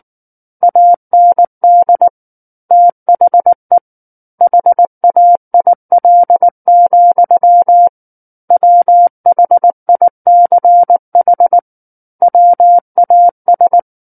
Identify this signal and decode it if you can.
Morse code, heard as a keyed tone, about 19 wpm, 700 Hz. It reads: AND THE HAIL, WHICH WAS
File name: Book One / 10-Chapter 10 - 19 WPM